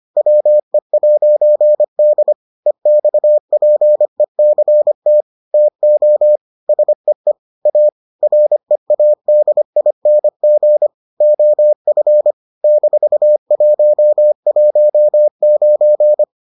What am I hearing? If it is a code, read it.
Morse code, heard as a keyed tone, about 25 wpm, 600 Hz. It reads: WE'D EXPECT TO SEE A READING OF -119